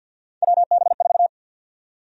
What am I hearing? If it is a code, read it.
Morse code, heard as a keyed tone, about 53 wpm, 700 Hz. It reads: WB4
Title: USA Call Sign Prefixes - Repeated 2x - 50wpm